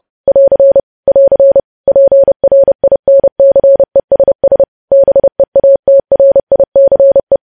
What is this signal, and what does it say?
Morse code, heard as a keyed tone, about 30 wpm, 550 Hz. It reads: <AR> <AR> PRINCESS BEATRICE
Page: news headlines in morse code at 30 wpm